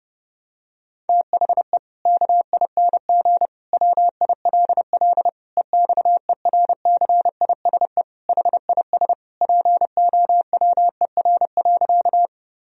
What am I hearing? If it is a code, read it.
Morse code, heard as a keyed tone, about 30 wpm, 700 Hz. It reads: THE KING WILL EXERCISE HIS POWER.
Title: Sentences from Top 1000 Words - Review Part 4 - 30wpm